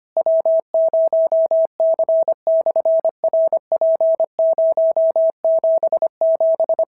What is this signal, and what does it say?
Morse code, heard as a keyed tone, about 25 wpm, 650 Hz. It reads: W0C/RP077
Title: SOTA QSO Practice 25wpm